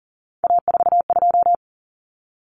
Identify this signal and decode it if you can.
Morse code, heard as a keyed tone, about 40 wpm, 700 Hz. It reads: A42
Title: Common Cut Numbers - 3-Digit 40wpm